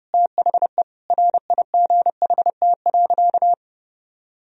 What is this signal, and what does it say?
Morse code, heard as a keyed tone, about 30 wpm, 700 Hz. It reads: THE RIGHT.
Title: Sentences from Top 1000 Words - Review Part 1 - 30wpm